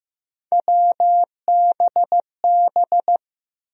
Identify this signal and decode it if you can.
Morse code, heard as a keyed tone, about 15 wpm, 700 Hz. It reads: WBB
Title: USA Call Sign Postfixes - 3 Letters 15wpm